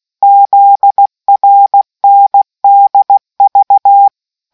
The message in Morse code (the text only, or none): ZRNDV